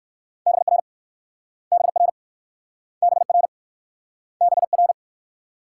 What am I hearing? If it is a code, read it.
Morse code, heard as a keyed tone, about 60 wpm, 700 Hz. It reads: BR BR BR BR